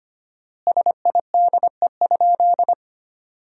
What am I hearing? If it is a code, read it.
Morse code, heard as a keyed tone, about 25 wpm, 700 Hz. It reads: SIDE?
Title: 4x Word-Spacing - Sentences from Top 400 Words - 25wpm